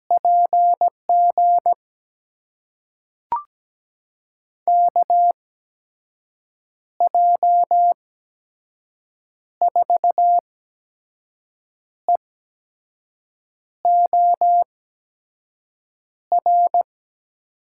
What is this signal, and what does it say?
Morse code, heard as a keyed tone, about 17 wpm, spaced as for 5 wpm, 700 Hz. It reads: PG KJ4EOR
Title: USA Call Signs - 6 Characters - 8x Character Spacing - 17wpm